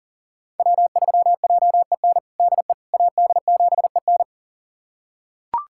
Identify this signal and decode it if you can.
Morse code, heard as a keyed tone, about 37 wpm, 700 Hz. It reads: W3JEN DE AD7EN
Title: USA Callsign DE Callsign - 5 Characters - 40wpm